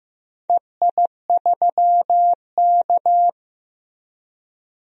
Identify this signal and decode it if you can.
Morse code, heard as a keyed tone, about 16 wpm, 700 Hz. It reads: EI3K
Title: DXCC-100 Call Signs - 4 Characters - Repeated 2x - 15wpm